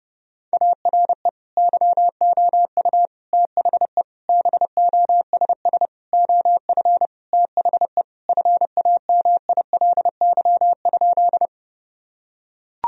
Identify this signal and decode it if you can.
Morse code, heard as a keyed tone, about 32 wpm, 700 Hz. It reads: ARE YOU THE BOSS OF THE FAMILY?
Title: Sentences from Top 1000 Words - Review Part 4 - 30wpm